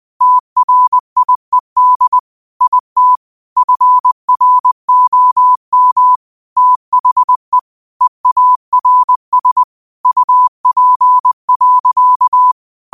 Morse code, heard as a keyed tone, about 20 wpm, 1000 Hz. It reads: TRIED IT FROM THE EARS UP.
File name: qotdfiles / 20wpm